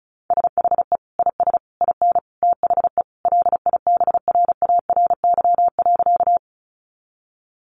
Morse code, heard as a keyed tone, about 34 wpm, 700 Hz. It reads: SHE IS IN THE LIBRARY.
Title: Sentences from Top 600 Words 35wpm